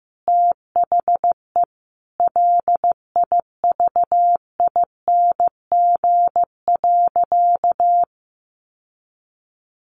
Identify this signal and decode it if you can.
Morse code, heard as a keyed tone, about 15 wpm, 700 Hz. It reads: THE LIVING.
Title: Sentences from Top 1100 Words 15wpm